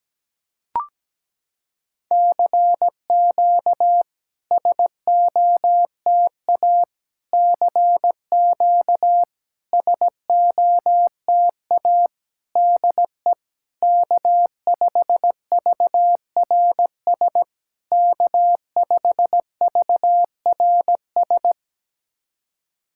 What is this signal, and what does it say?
Morse code, heard as a keyed tone, about 17 wpm, 700 Hz. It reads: CQ SOTA CQ SOTA DE K5VRS K5VRS
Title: SOTA QSO Practice 17wpm